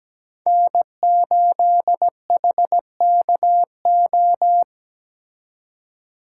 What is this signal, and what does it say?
Morse code, heard as a keyed tone, about 20 wpm, 700 Hz. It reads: N8HKO
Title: Speed Racing - USA 1x3 Call Signs 17wpm